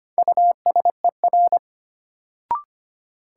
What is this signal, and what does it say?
Morse code, heard as a keyed tone, about 25 wpm, 700 Hz. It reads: USER
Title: Sentences from Top 1000 Words - Review Part 5 - New Word Drill - 25wpm